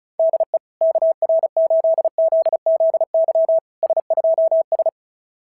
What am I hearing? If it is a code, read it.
Morse code, heard as a keyed tone, about 33 wpm, 660 Hz. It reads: DE KR8ZZY S2S
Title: SOTA QSO Practice 35wpm